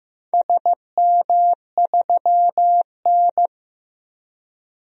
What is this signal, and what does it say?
Morse code, heard as a keyed tone, about 15 wpm, 700 Hz. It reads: SM3N
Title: DXCC-100 Call Signs - 4 Characters - 15wpm